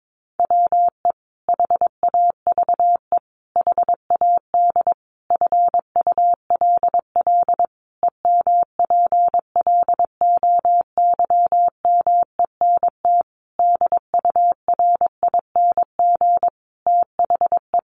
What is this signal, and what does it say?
Morse code, heard as a keyed tone, about 22 wpm, 700 Hz. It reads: WE HAVE HAD FULL EMPLOYMENT DURING THE
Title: Sentences from Top 1500 Words 22wpm